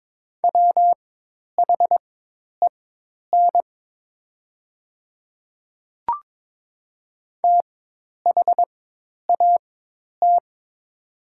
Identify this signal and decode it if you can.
Morse code, heard as a keyed tone, about 22 wpm, spaced as for 10 wpm, 700 Hz. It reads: WHEN THAT
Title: 4x Character Spacing - Top 100 Words - 22wpm